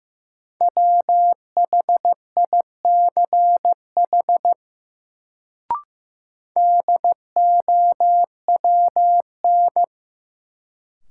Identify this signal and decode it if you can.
Morse code, heard as a keyed tone, about 15 wpm, 700 Hz. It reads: WHICH DOWN
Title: Sets of 2 Words - New Word Drill - 15wpm